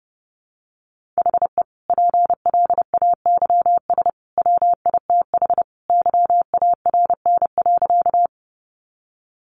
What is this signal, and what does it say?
Morse code, heard as a keyed tone, about 30 wpm, 700 Hz. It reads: HE PLAYS WITH YARN.
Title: Easy Phrases 30wpm